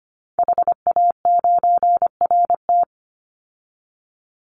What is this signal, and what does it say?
Morse code, heard as a keyed tone, about 25 wpm, 700 Hz. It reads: HA9RT
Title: Call Signs Repeated 3x - 25wpm